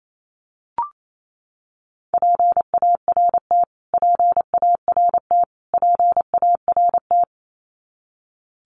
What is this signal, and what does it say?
Morse code, heard as a keyed tone, about 28 wpm, 700 Hz. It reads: PART PART PART
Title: Top 300 Words Repeated 28wpm